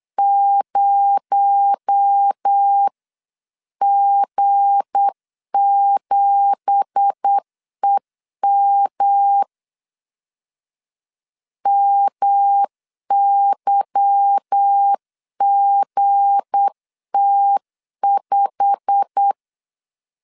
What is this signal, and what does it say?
Morse code, heard as a keyed tone, about 8 wpm, 800 Hz. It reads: 0G7EM MYGT5
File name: CW / DET / 1 9 1500C S A 8wpm 5637